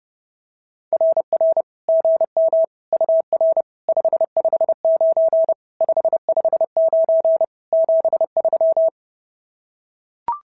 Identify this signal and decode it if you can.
Morse code, heard as a keyed tone, about 30 wpm, 650 Hz. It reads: RR GM UR 559 559 73
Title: SOTA QSO Practice 30wpm